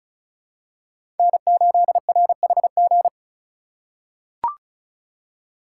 Morse code, heard as a keyed tone, about 34 wpm, 700 Hz. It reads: N8RHG
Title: USA Call Signs - 5 Characters - 6x Character Spacing - 35wpm